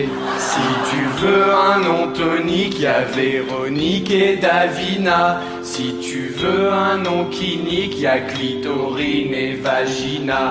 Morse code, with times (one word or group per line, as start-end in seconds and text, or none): none